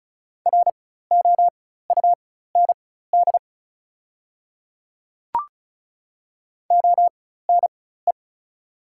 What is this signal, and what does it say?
Morse code, heard as a keyed tone, about 33 wpm, spaced as for 16 wpm, 690 Hz. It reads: ROUND ONE
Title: 4x Character Spacing - Top 200 Words - 35wpm